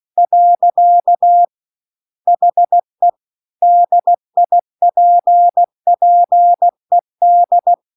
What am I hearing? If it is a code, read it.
Morse code, heard as a keyed tone, about 16 wpm, 700 Hz. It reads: . HE DIPPED